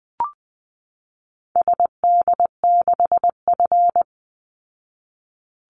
Morse code, heard as a keyed tone, about 20 wpm, 700 Hz. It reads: SD6F
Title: Call Signs Repeated 2x - 20wpm